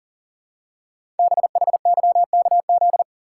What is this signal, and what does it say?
Morse code, heard as a keyed tone, about 39 wpm, 700 Hz. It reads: BHYKZ
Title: Code Groups - 5 Letters - 40wpm